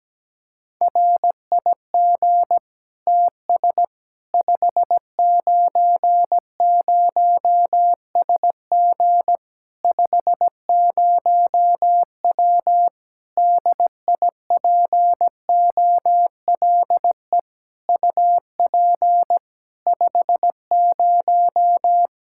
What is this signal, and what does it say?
Morse code, heard as a keyed tone, about 17 wpm, 700 Hz. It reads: RIG TS 590SG 50W DIPOLE UP 50
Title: Practice QSO - Exchange 2 - Info - 17wpm